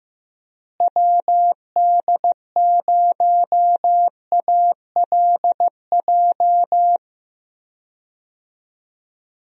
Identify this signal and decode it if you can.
Morse code, heard as a keyed tone, about 15 wpm, 700 Hz. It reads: WD0ALJ
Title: USA Call Signs - 6 Characters - Repeated 3x - 15wpm